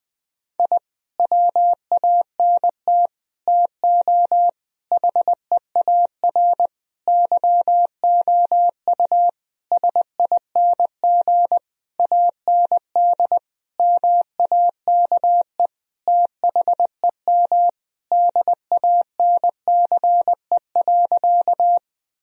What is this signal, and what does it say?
Morse code, heard as a keyed tone, about 20 wpm, 700 Hz. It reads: I WANT TO HEAR YOU SING AND MAKE THEM DANCE.
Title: Sentences from Top 1000 Words - Review Part 4 - 20wpm